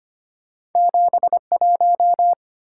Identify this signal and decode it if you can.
Morse code, heard as a keyed tone, about 34 wpm, 700 Hz. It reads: 71